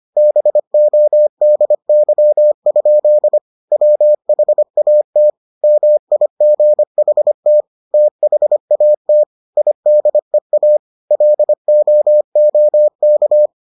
Morse code, heard as a keyed tone, about 25 wpm, 600 Hz. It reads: BODY? WHAT MIGHT THAT IDEA LOOK